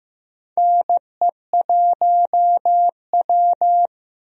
Morse code, heard as a keyed tone, about 15 wpm, 700 Hz. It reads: NE1W
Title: USA Call Signs - 4 Characters - 15wpm